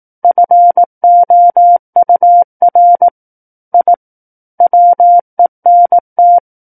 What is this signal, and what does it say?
Morse code, heard as a keyed tone, about 18 wpm, 700 Hz. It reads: FOUR I WENT
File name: 18 WPM